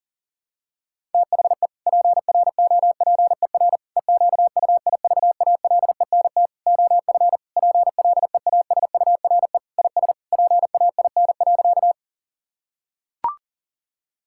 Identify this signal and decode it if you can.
Morse code, heard as a keyed tone, about 40 wpm, 700 Hz. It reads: THE PROPER EQUIVALENT OF PLEASURE IS PAIN.
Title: Sentences from Top 1600 Words 40wpm